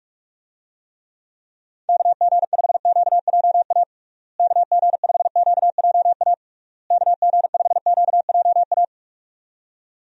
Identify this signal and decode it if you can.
Morse code, heard as a keyed tone, about 41 wpm, 700 Hz. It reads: KG5QJA KG5QJA KG5QJA
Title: USA Call Signs - 6 Characters - Repeated 3x - 45wpm